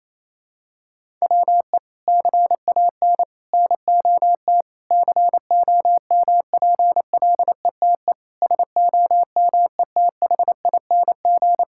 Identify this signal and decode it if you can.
Morse code, heard as a keyed tone, about 28 wpm, 700 Hz. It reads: WE CAN NOT COMPLETE SOMETHING
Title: Sentences from Top 1000 Words - Review Part 6 - 28wpm